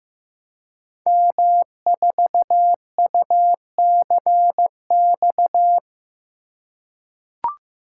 Morse code, heard as a keyed tone, about 15 wpm, 700 Hz. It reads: M4UCX
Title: Code Groups - 5 Letter:Numbers - 15wpm